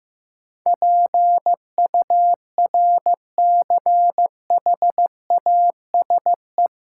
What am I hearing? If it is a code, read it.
Morse code, heard as a keyed tone, about 15 wpm, 700 Hz. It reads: PURCHASE
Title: Sentences from Top 800 Words - New Word Drill - 15wpm